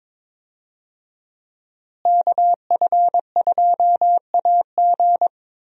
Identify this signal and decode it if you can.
Morse code, heard as a keyed tone, about 21 wpm, 700 Hz. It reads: KF2AG
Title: USA Call Signs - 5 Characters - Repeated 3x - 22wpm